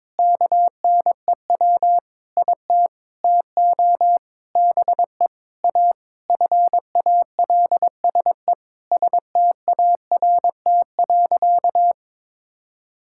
Morse code, heard as a keyed tone, about 21 wpm, 700 Hz. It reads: KNEW IT TO BE A FALSE START.
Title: Sentences from Top 1600 Words 22wpm